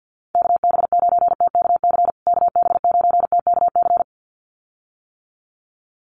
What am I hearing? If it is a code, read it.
Morse code, heard as a keyed tone, about 50 wpm, 700 Hz. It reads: KB9TKC KB9TKC
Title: USA Call Signs Repeated 2x 50wpm